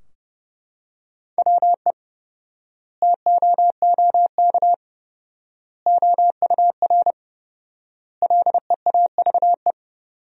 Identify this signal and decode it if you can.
Morse code, heard as a keyed tone, about 30 wpm, 700 Hz. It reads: WE TOOK OUR LEAVE
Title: 4x Word-Spacing - Sentences from Top 400 Words - 30wpm